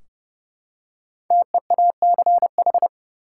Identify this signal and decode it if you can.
Morse code, heard as a keyed tone, about 30 wpm, 700 Hz. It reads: TEACH
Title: Five Letter Words - 30wpm